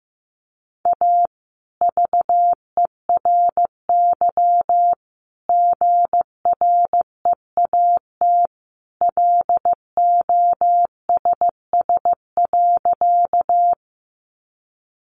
Morse code, heard as a keyed tone, about 15 wpm, 700 Hz. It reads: A VERY GREAT LOSS.
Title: Sentences from Top 1400 Words 15wpm